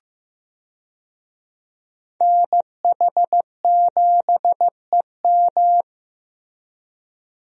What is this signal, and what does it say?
Morse code, heard as a keyed tone, about 15 wpm, 700 Hz. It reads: NH7EM
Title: USA Call Signs - 5 Characters - Repeated 3x - 15wpm